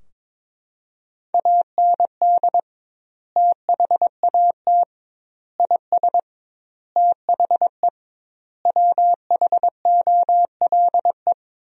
Morse code, heard as a keyed tone, about 22 wpm, 700 Hz. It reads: AND THAT IS THE WHOLE
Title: 2x Word-Spacing - Sentences from Top 400 Words - 22wpm